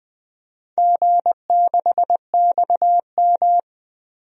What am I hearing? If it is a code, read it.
Morse code, heard as a keyed tone, about 20 wpm, 700 Hz. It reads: G6XM
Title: DXCC-100 Call Signs - 4 Characters - 20wpm